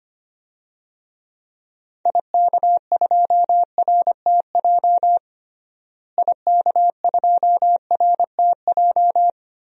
Morse code, heard as a keyed tone, about 25 wpm, 700 Hz. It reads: IK2RTJ IK2RTJ
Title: DXCC-100 Call Signs - 6 Characters - Repeated 2x - 25wpm